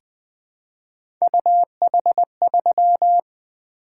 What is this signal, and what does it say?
Morse code, heard as a keyed tone, about 20 wpm, 700 Hz. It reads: UH3
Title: Three Letter-Number Combinations - 20wpm